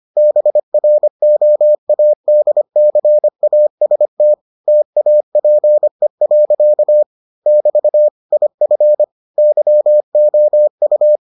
Morse code, heard as a keyed tone, about 25 wpm, 600 Hz. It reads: BROADCAST TAPE. <BT> IF YOU